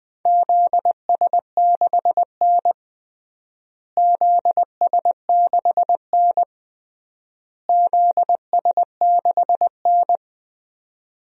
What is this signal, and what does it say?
Morse code, heard as a keyed tone, about 20 wpm, 700 Hz. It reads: ZS6N ZS6N ZS6N
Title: DXCC-100 Call Signs - 4 Characters - Repeated 3x - 20wpm